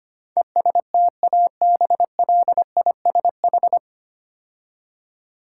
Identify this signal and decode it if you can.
Morse code, heard as a keyed tone, about 25 wpm, 700 Hz. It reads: ESTABLISH
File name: practice / 25wpm / 9 Letter Words - First 3 Letters Given - 25wpm